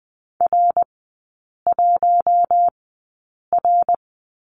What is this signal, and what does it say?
Morse code, heard as a keyed tone, about 20 wpm, spaced as for 8 wpm, 700 Hz. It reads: R1R
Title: 060 Lesson 11e - Letter h - Three Characters 20wpm